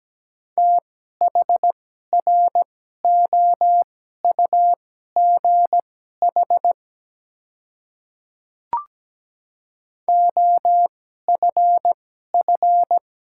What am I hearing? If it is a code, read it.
Morse code, heard as a keyed tone, about 17 wpm, spaced as for 12 wpm, 700 Hz. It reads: THROUGH OFF
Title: 2x Character Spacing - Top 200 Words - 17wpm